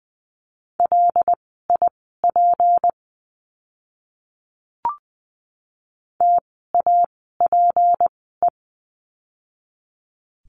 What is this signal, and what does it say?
Morse code, heard as a keyed tone, about 20 wpm, spaced as for 14 wpm, 700 Hz. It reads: LIP TAPE
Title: 153 Lesson 25f - Letter p - Words 20wpm